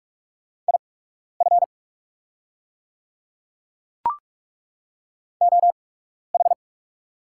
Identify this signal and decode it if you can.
Morse code, heard as a keyed tone, about 43 wpm, spaced as for 12 wpm, 690 Hz. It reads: IF OH
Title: 8x Character Spacing - Top 500 Words - 45wpm